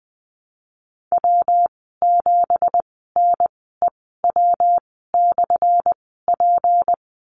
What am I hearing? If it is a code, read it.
Morse code, heard as a keyed tone, about 20 wpm, spaced as for 15 wpm, 700 Hz. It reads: W7NEW/P
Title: 179 Lesson 29g - Slash - Call Signs 20wpm